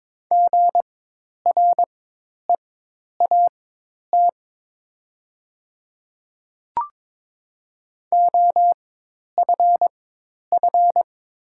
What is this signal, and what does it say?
Morse code, heard as a keyed tone, about 22 wpm, spaced as for 10 wpm, 690 Hz. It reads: GREAT OFF
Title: 4x Character Spacing - Top 300 Words - 22wpm